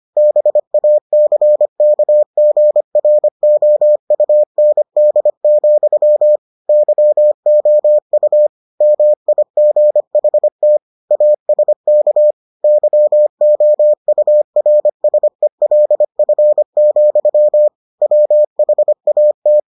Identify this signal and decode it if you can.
Morse code, heard as a keyed tone, about 25 wpm, 600 Hz. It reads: BACKGROUND, YOU MIGHT ASK YOURSELF, WHAT